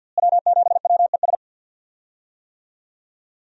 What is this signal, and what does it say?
Morse code, heard as a keyed tone, about 53 wpm, 700 Hz. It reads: W7WES
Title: USA Call Signs - 5 Characters - Repeated 2x - 50wpm